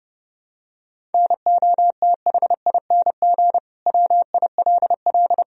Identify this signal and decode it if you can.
Morse code, heard as a keyed tone, about 30 wpm, 700 Hz. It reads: NOTHING WILL
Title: Sentences from Top 1000 Words - Review Part 1 - 30wpm